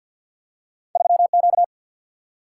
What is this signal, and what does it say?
Morse code, heard as a keyed tone, about 47 wpm, 700 Hz. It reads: 3 Q
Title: Single-Letter Single-Number 50wpm